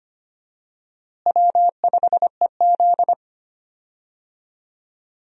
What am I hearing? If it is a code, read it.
Morse code, heard as a keyed tone, about 25 wpm, 700 Hz. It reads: W5EZ
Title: USA Call Signs - 4 Characters - Repeated 3x - 25wpm